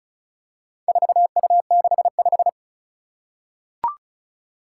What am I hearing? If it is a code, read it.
Morse code, heard as a keyed tone, about 36 wpm, 700 Hz. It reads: 4U65
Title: Four Challenging Letter-Number Combinations 35wpm